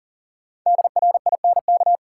Morse code, heard as a keyed tone, about 40 wpm, 700 Hz. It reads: DRINK
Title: Sentences from Top 1500 Words - New Word Drill Alt - 40wpm